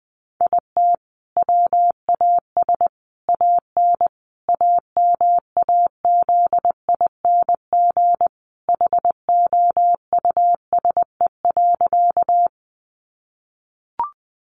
Morse code, heard as a keyed tone, about 20 wpm, 700 Hz. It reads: IT WAS AN AMAZING HOUSE.
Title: Sentences from Top 1500 Words 20wpm